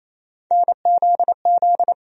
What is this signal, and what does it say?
Morse code, heard as a keyed tone, about 28 wpm, 700 Hz. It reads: NZZ
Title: USA Call Sign Postfixes - 3 Letters 28wpm